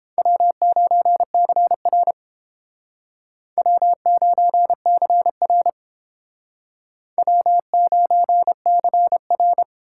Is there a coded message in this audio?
Yes: Morse code from 0.2 to 9.6 s, heard as a keyed tone, about 31 wpm, 700 Hz. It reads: W9CR W9CR W9CR